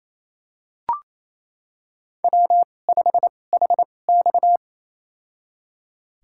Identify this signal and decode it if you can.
Morse code, heard as a keyed tone, about 28 wpm, spaced as for 20 wpm, 700 Hz. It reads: W5HX
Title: USA Call Signs - 4 Characters - 2x Character Spacing - 28wpm